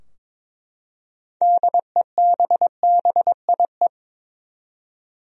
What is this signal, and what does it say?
Morse code, heard as a keyed tone, about 22 wpm, 700 Hz. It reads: DEBBIE
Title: USA Common Names - 22wpm